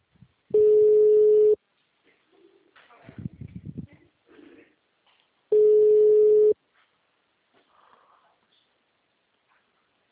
Ringback tone (Europe, 425 Hz).
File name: out- -2025- - - .295